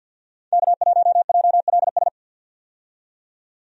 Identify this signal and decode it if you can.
Morse code, heard as a keyed tone, about 47 wpm, 700 Hz. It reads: K1JLS
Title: USA Call Signs - 5 Characters - 50wpm